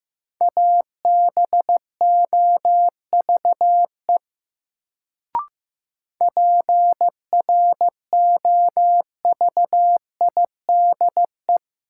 Morse code, heard as a keyed tone, about 15 wpm, 700 Hz. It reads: ABOVE PROVIDE
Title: Sentences from Top 1000 Words - Review Part 2 - New Word Drill - 15wpm